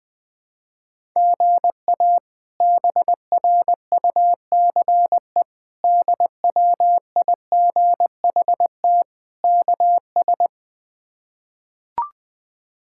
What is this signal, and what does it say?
Morse code, heard as a keyed tone, about 20 wpm, 700 Hz. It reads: GA BRUCE DWIGHT KS
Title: K1USN SST QSO Practice - 20wpm